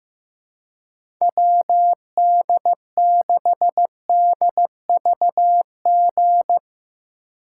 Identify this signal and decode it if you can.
Morse code, heard as a keyed tone, about 15 wpm, 700 Hz. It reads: WD6DVG
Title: USA Call Signs - 3x Character Spacing - 15wpm